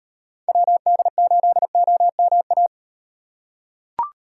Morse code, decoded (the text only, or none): WD8OMA